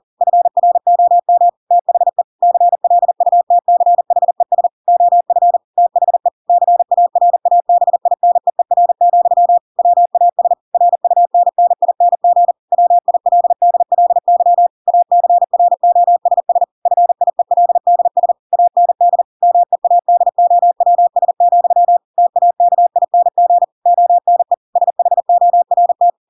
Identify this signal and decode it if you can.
Morse code, heard as a keyed tone, about 40 wpm, 700 Hz. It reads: FROM THE CLUTCHES OF THE CARABINEER, WAS RUNNING WILDLY ACROSS FIELDS AND MEADOWS, TAKING ONE SHORT
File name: Pinocchio / Pinocchio ch4 40 40 ik1hge